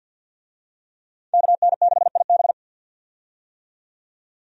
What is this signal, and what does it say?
Morse code, heard as a keyed tone, about 53 wpm, 700 Hz. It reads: KN6IB
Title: USA Call Signs - 5 Characters - 50wpm